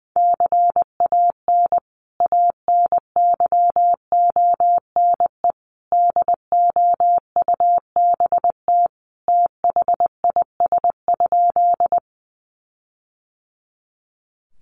Morse code, heard as a keyed tone, about 20 wpm, 700 Hz. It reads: CAN ANYONE DOUBT THIS?
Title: Sentences from Top 1500 Words 20wpm